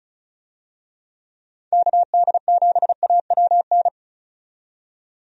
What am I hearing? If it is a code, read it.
Morse code, heard as a keyed tone, about 34 wpm, 700 Hz. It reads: KD7AWN
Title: USA Call Signs - 6 Characters - 35wpm